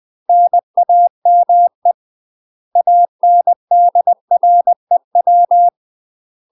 Morse code, heard as a keyed tone, about 20 wpm, 700 Hz. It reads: NAME ANDREW
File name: UK MASTER QSOs TRAINING / 20WPM / UK MASTER QSOs 0020